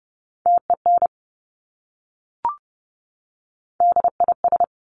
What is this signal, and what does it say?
Morse code, heard as a keyed tone, about 30 wpm, 690 Hz. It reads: TEN DIS
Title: Top Three Letter Combinations in Words 30wpm